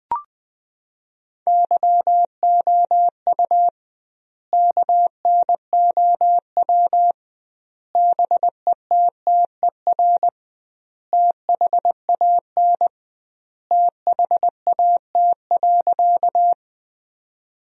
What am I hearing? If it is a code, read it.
Morse code, heard as a keyed tone, about 20 wpm, 700 Hz. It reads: YOU KNOW BETTER THAN THAT.
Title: 2x Word-Spacing - Sentences from Top 400 Words - 20wpm